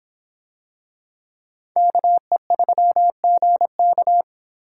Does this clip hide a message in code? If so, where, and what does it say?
Yes: Morse code from 1.8 to 4.2 s, heard as a keyed tone, about 24 wpm, 700 Hz. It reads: KE3GK